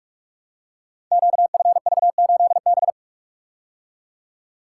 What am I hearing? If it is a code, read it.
Morse code, heard as a keyed tone, about 44 wpm, 700 Hz. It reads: QFV8B
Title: Code Groups - 5 Letter:Numbers - 45wpm